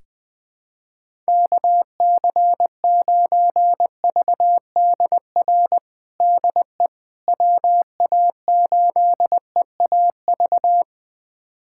Morse code, heard as a keyed tone, about 21 wpm, 700 Hz. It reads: KC9VDR DE WA8EAV
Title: USA Callsign DE Callsign 20wpm